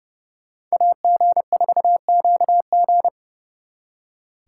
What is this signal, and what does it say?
Morse code, heard as a keyed tone, about 38 wpm, 700 Hz. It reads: AG4QG